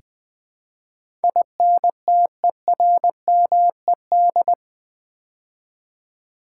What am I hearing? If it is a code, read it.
Morse code, heard as a keyed tone, about 20 wpm, 700 Hz. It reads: INTERMED
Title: Build A Word - 4 Letters At A Time - 12-Letter Words - 20wpm